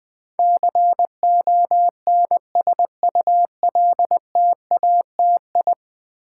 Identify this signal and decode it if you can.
Morse code, heard as a keyed tone, about 20 wpm, 700 Hz. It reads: CONSULTATI
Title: Build A Word - 5 Letters At A Time - 12-Letter Words - 20wpm